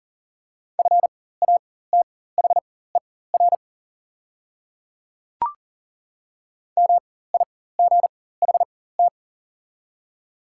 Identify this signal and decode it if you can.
Morse code, heard as a keyed tone, about 40 wpm, spaced as for 19 wpm, 700 Hz. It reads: FATHER MIGHT
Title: 4x Character Spacing - Top 300 Words - 40wpm